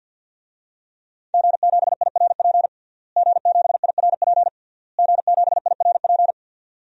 Morse code, heard as a keyed tone, about 47 wpm, 700 Hz. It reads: G7IRP G7IRP G7IRP
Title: DXCC-100 Call Signs - 5 Characters - Repeated 3x - 50wpm